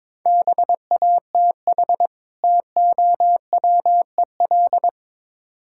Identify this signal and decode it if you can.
Morse code, heard as a keyed tone, about 22 wpm, 700 Hz. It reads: BATH TOWEL
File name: Easy Words - Speed-Racing with Hints - 22wpm